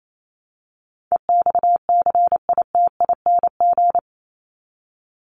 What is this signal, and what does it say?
Morse code, heard as a keyed tone, about 28 wpm, 700 Hz. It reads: EXCITING
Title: Top 1751-2000 Words 28wpm